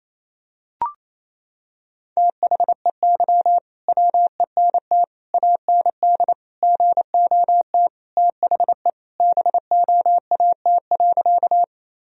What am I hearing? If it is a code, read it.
Morse code, heard as a keyed tone, about 28 wpm, 700 Hz. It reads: THEY WENT AND GOT THE BOAT.
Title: Sentences from Top 1000 Words - Review Part 3 - 28wpm